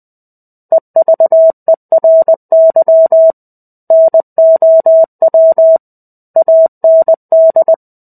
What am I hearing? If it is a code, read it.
Morse code, heard as a keyed tone, about 20 wpm, 650 Hz. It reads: EVERY NOW AND
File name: 05-Chapter-05 20 WPM